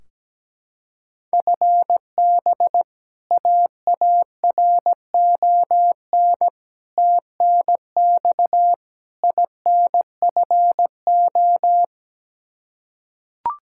Morse code, heard as a keyed tone, about 17 wpm, 700 Hz. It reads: FB AARON TNX INFO
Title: K1USN SST QSO Practice 17wpm